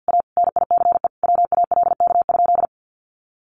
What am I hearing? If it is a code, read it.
Morse code, heard as a keyed tone, about 50 wpm, 700 Hz. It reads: A NICE WALK?
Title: Sentences from Top 1100 Words 50wpm